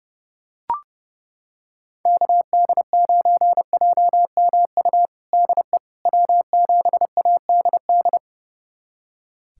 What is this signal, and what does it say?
Morse code, heard as a keyed tone, about 30 wpm, 700 Hz. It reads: KD9JMU DE W7ADD
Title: USA Callsign DE Callsign 30wpm